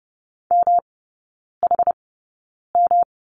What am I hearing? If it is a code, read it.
Morse code, heard as a keyed tone, about 30 wpm, spaced as for 9 wpm, 700 Hz. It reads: MHM
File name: Lesson 18e - Letter m - Three Characters 30wpm